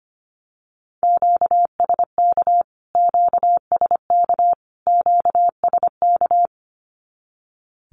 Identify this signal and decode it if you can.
Morse code, heard as a keyed tone, about 25 wpm, 700 Hz. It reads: QSK QSK QSK
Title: Q-Codes Repeated 3x 25wpm